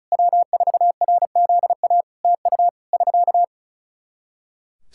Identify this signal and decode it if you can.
Morse code, heard as a keyed tone, about 34 wpm, 700 Hz. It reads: W4RZA TU <SK>
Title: Practice QSO - Exchange 4 - Ending - 35wpm